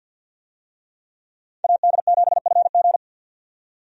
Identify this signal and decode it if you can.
Morse code, heard as a keyed tone, about 48 wpm, 700 Hz. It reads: AD7FG
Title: USA Call Signs - 5 Characters - 50wpm